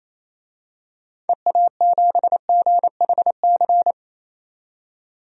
Morse code, heard as a keyed tone, about 28 wpm, 700 Hz. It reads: EA7GHC
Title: DXCC-100 Call Signs - 6 Characters - 28wpm